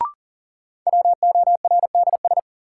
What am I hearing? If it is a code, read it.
Morse code, heard as a keyed tone, about 40 wpm, 700 Hz. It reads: WORDS
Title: Sentences from Top 1000 Words - Review Part 6 - New Word Drill - 40wpm